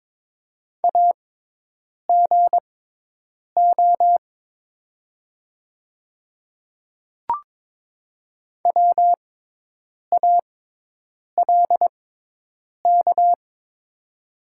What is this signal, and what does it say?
Morse code, heard as a keyed tone, about 22 wpm, spaced as for 8 wpm, 700 Hz. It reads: AGO WALK
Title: 6x Character Spacing - Top 500 Words - 22wpm